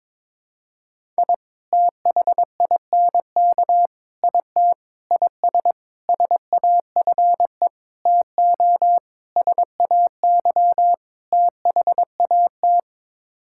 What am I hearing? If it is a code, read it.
Morse code, heard as a keyed tone, about 22 wpm, 700 Hz. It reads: I THINK IT IS SAFE TO SAY THAT
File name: practice / 22wpm / Sets of 8 Words - 22wpm